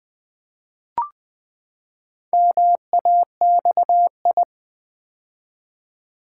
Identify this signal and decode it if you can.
Morse code, heard as a keyed tone, about 20 wpm, 700 Hz. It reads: MAXI
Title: Build A Word - 4 Letters At A Time - 8-Letter Words - 20wpm